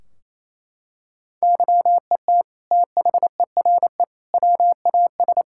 Morse code, heard as a keyed tone, about 28 wpm, 700 Hz. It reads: YET THERE WAS
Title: Sentences from Top 1600 Words 28wpm